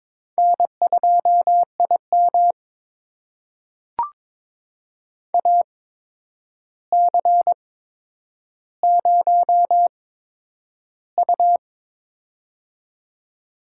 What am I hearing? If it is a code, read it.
Morse code, heard as a keyed tone, about 22 wpm, spaced as for 9 wpm, 700 Hz. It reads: N2IM AC0U
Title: USA Call Signs - 8x Character Spacing - 22wpm